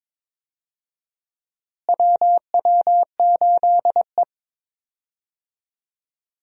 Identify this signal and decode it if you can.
Morse code, heard as a keyed tone, about 21 wpm, 700 Hz. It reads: WW8E
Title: USA Call Signs - 4 Characters - Repeated 3x - 22wpm